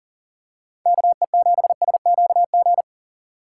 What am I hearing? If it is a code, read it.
Morse code, heard as a keyed tone, about 40 wpm, 700 Hz. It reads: KE7SQG
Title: USA Call Signs - 3x Character Spacing - 40wpm